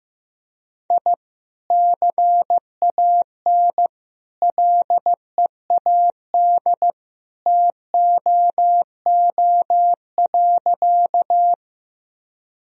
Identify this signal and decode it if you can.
Morse code, heard as a keyed tone, about 15 wpm, 700 Hz. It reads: I CAN LEAD TOO.
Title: Sentences from Top 1000 Words - Review Part 2 - 15wpm